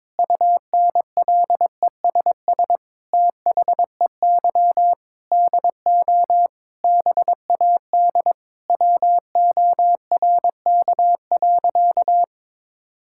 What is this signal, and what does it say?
Morse code, heard as a keyed tone, about 22 wpm, 700 Hz. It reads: UNLESS THEY DO BAD WORK.
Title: Sentences from Top 1000 Words - Review Part 6 - 22wpm